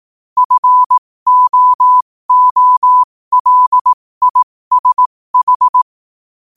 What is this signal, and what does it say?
Morse code, heard as a keyed tone, about 18 wpm, spaced as for 15 wpm, 1000 Hz. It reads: FOOLISH
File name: qotdfiles / 15wpm